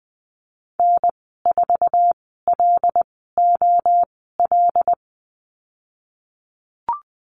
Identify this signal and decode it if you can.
Morse code, heard as a keyed tone, about 20 wpm, spaced as for 15 wpm, 700 Hz. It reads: N4LOL
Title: 193 Lesson 31g - Letter v - Call Signs 20wpm